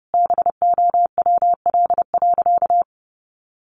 Morse code, heard as a keyed tone, about 30 wpm, 700 Hz. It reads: BOWL.
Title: Sentences from Top 1200 Words 30wpm